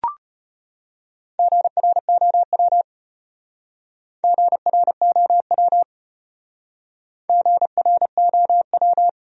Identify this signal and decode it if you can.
Morse code, heard as a keyed tone, about 33 wpm, 700 Hz. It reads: GROW GROW GROW